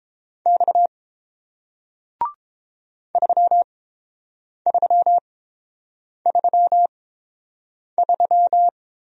Morse code, heard as a keyed tone, about 30 wpm, spaced as for 7 wpm, 700 Hz. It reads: X 3333